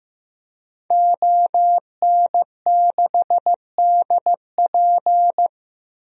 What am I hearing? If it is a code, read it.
Morse code, heard as a keyed tone, about 15 wpm, 700 Hz. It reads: ON6DP
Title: DXCC-100 Call Signs - 5 Characters - 15wpm